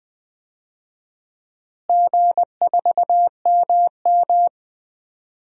Morse code, heard as a keyed tone, about 21 wpm, 700 Hz. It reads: G4MM
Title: DXCC-100 Call Signs - 4 Characters - Repeated 2x - 20wpm